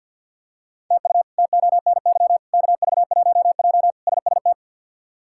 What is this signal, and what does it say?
Morse code, heard as a keyed tone, about 47 wpm, 700 Hz. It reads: TU TONY KV1J SST
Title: K1USN SST QSO Practice - 50wpm